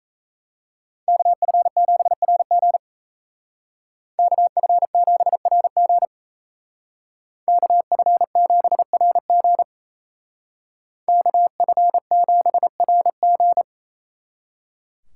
Morse code, heard as a keyed tone, about 36 wpm, 700 Hz. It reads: KF7RG KF7RG KF7RG KF7RG